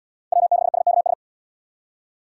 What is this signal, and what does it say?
Morse code, heard as a keyed tone, about 69 wpm, 700 Hz. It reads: W6SKS